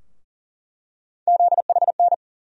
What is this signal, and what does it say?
Morse code, heard as a keyed tone, about 40 wpm, 700 Hz. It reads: ZHN
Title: USA Call Sign Postfixes - 3 Letters 40wpm